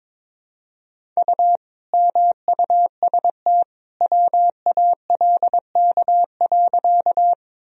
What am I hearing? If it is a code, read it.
Morse code, heard as a keyed tone, about 22 wpm, 700 Hz. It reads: U MUST WALK.
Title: Sentences from Top 300 Words and Abbreviations 22wpm